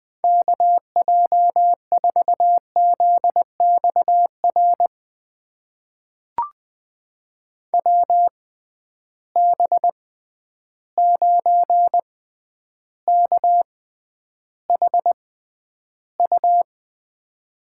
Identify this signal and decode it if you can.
Morse code, heard as a keyed tone, about 20 wpm, spaced as for 10 wpm, 700 Hz. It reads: KJ4ZXR WB9KHU
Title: USA Call Signs - 6 Characters - 6x Character Spacing - 20wpm